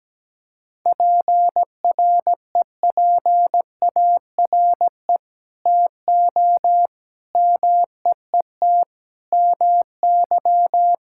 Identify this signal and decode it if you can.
Morse code, heard as a keyed tone, about 17 wpm, 700 Hz. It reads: PREPARE TO MEET MY
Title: Sentences from Top 1000 Words - Review Part 6 - 17wpm